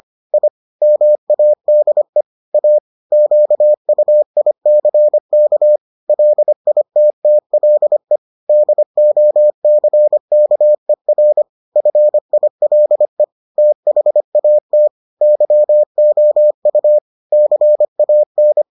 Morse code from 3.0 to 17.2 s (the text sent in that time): QUICK LITTLE DOCKER FILE THAT YOU